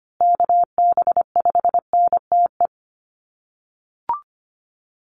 Morse code, heard as a keyed tone, about 25 wpm, 700 Hz. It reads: KB5NTE
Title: USA Call Signs Repeated 3x 25wpm